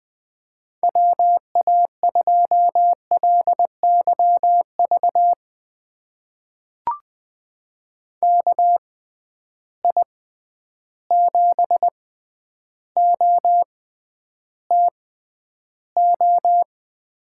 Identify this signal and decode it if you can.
Morse code, heard as a keyed tone, about 20 wpm, spaced as for 10 wpm, 700 Hz. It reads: WA2LYV KI7OTO
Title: USA Call Signs - 6 Characters - 6x Character Spacing - 20wpm